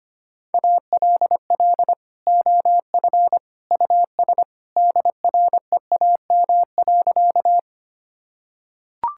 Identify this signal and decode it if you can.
Morse code, heard as a keyed tone, about 25 wpm, 700 Hz. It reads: ALL OF US DREAM.
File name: practice / 25wpm / Sentences with 2-Word Phrases - 25wpm